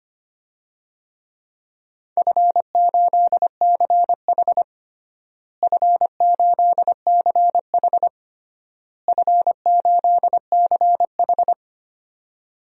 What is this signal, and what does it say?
Morse code, heard as a keyed tone, about 25 wpm, 700 Hz. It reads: F8CH F8CH F8CH
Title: DXCC-100 Call Signs - 4 Characters - Repeated 3x - 25wpm